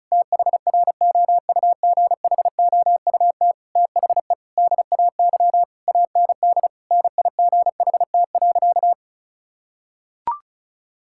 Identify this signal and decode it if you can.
Morse code, heard as a keyed tone, about 36 wpm, 700 Hz. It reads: THROUGHOUT THE DAY AND NIGHT.
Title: Sentences from Top 1000 Words - Review Part 6 - 35wpm